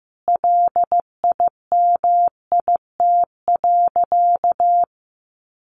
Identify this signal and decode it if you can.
Morse code, heard as a keyed tone, about 15 wpm, 700 Hz. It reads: LIMIT.
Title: Sentences from Top 1500 Words 15wpm